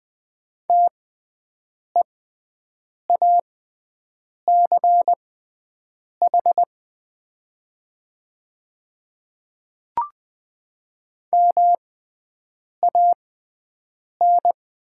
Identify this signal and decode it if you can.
Morse code, heard as a keyed tone, about 20 wpm, spaced as for 7 wpm, 700 Hz. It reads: TEACH MAN
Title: 6x Character Spacing - Top 500 Words - 20wpm